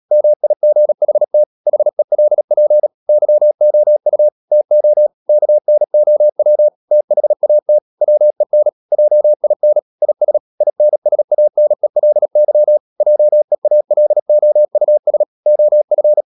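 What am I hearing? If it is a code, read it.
Morse code, heard as a keyed tone, about 37 wpm, 600 Hz. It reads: MIGHT HELP YOU TO KNOW THAT WEN JIN IS INSANELY JEALOUS OF
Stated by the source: Morse code practice at 35–39 wpm sending English text of Ken Follett's novel Never, from CW med Gnister.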